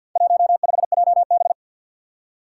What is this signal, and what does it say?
Morse code, heard as a keyed tone, about 51 wpm, 700 Hz. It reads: 15JB
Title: Four Challenging Letter-Number Combinations 50wpm